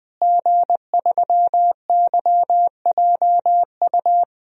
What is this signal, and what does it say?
Morse code, heard as a keyed tone, about 20 wpm, 700 Hz. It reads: G3YJU
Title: DXCC-100 Call Signs - 5 Characters - Repeated 3x - 20wpm